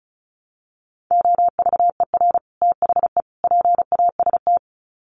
Morse code, heard as a keyed tone, about 34 wpm, 700 Hz. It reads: OVER THE PAST
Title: Sets of 3 Words 35wpm